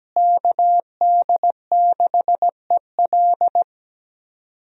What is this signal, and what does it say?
Morse code, heard as a keyed tone, about 17 wpm, 700 Hz. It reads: KD6EL
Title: USA Call Signs - 5 Characters - Repeated 2x - 17wpm